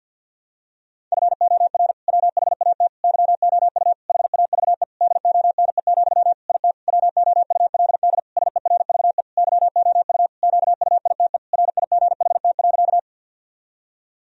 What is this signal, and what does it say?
Morse code, heard as a keyed tone, about 52 wpm, 700 Hz. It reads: FOR WHAT YOU HAVE DONE, IT WOULD SERVE YOU QUITE RIGHT.
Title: Sentences from Top 1000 Words - Review Part 2 - 50wpm